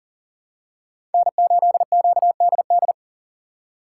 Morse code, heard as a keyed tone, about 40 wpm, 700 Hz. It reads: N8QDD
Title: USA Call Signs - 5 Characters - 2x Character Spacing - 40wpm